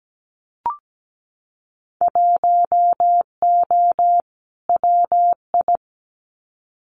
Morse code, heard as a keyed tone, about 17 wpm, 700 Hz. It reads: 1O WI
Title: Winter Field Day Exchanges 17wpm